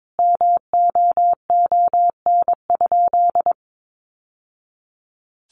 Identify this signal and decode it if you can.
Morse code, heard as a keyed tone, about 22 wpm, 700 Hz. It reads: MOON?
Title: Sentences from Top 1200 Words 22wpm